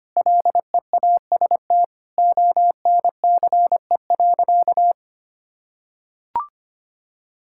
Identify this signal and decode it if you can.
Morse code, heard as a keyed tone, about 25 wpm, 700 Hz. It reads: LEAST ONCE.
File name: practice / 25wpm / Sentences with 2-Word Phrases - 25wpm